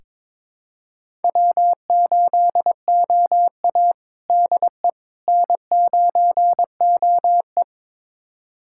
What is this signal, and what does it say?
Morse code, heard as a keyed tone, about 22 wpm, 700 Hz. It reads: W8OA DE N9OE
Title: USA Callsign DE Callsign - 4 Characters - 22wpm